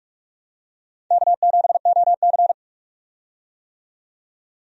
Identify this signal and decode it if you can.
Morse code, heard as a keyed tone, about 46 wpm, 700 Hz. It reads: K7OC
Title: USA Call Signs - 4 Characters - 45wpm